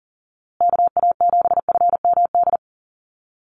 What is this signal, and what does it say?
Morse code, heard as a keyed tone, about 40 wpm, 700 Hz. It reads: KA7FMD
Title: USA Call Signs Repeated 2x 40wpm